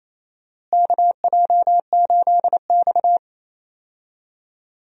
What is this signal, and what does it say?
Morse code, heard as a keyed tone, about 28 wpm, 700 Hz. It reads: KJ8X
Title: USA Call Signs - 4 Characters - Repeated 3x - 28wpm